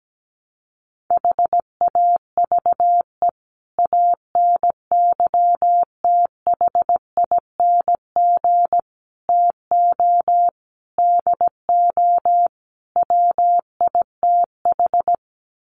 Morse code, heard as a keyed tone, about 17 wpm, 700 Hz. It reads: HAVE ANYTHING TO DO WITH
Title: Sets of 5 Words 17wpm